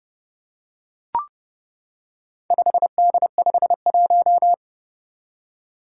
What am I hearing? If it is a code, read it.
Morse code, heard as a keyed tone, about 31 wpm, 700 Hz. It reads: 5D51
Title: Four Challenging Letter-Number Combinations 30wpm